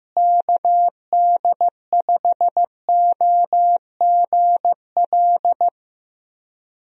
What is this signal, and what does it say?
Morse code, heard as a keyed tone, about 15 wpm, 700 Hz. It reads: KD5OGL
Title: USA Call Signs - 6 Characters - Repeated 3x - 15wpm